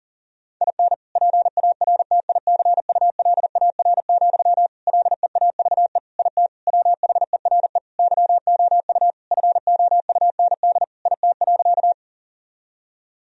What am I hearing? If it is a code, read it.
Morse code, heard as a keyed tone, about 40 wpm, 700 Hz. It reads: IN PARTICULAR, LEAVE IT WHERE YOU FOUND IT.
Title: Sentences from Top 1000 Words - Review Part 4 - 40wpm